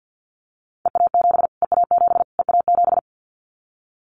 Morse code, heard as a keyed tone, about 49 wpm, 700 Hz. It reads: EA7 EA7 EA7
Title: Callsign Prefixes for Top 25 DXCC - Repeated 3x - 50wpm